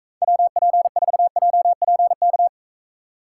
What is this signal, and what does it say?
Morse code, heard as a keyed tone, about 43 wpm, 700 Hz. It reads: WP4JPK